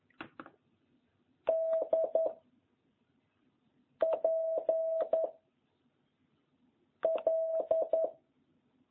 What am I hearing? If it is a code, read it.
Morse code, heard as a keyed tone, about 10 wpm, 650 Hz. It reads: D P L